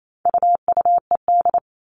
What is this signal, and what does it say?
Morse code, heard as a keyed tone, about 28 wpm, 700 Hz. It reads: UUED
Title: All Cut Numbers - 4-Digit 28wpm